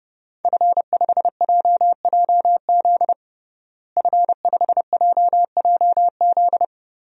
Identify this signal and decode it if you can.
Morse code, heard as a keyed tone, about 30 wpm, 700 Hz. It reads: F5JJZ F5JJZ
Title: DXCC-100 Call Signs - 5 Characters - Repeated 3x - 30wpm